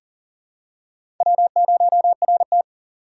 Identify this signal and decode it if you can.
Morse code, heard as a keyed tone, about 38 wpm, 700 Hz. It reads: W0RT
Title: USA Call Signs - 4 Characters - 6x Character Spacing - 40wpm